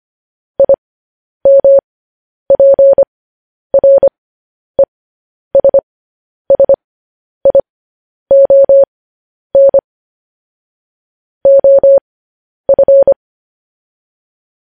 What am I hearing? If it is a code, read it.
Morse code, heard as a keyed tone, about 25 wpm, spaced as for 10 wpm, 550 Hz. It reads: IMPRESSION OF
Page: News Headlines in Morse Code at 10 WPM